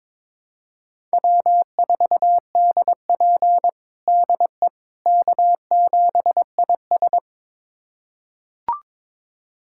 Morse code, heard as a keyed tone, about 21 wpm, 700 Hz. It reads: W4DP DE K7IS
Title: USA Callsign DE Callsign - 4 Characters - 22wpm